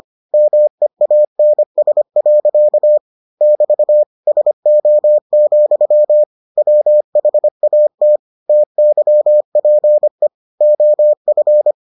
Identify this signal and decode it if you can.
Morse code, heard as a keyed tone, about 25 wpm, 600 Hz. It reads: MEANS. <BT> SO, WHAT TYPE OF